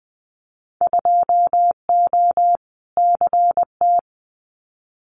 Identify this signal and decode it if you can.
Morse code, heard as a keyed tone, about 20 wpm, 700 Hz. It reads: 2O CT
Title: Winter Field Day Exchanges 20wpm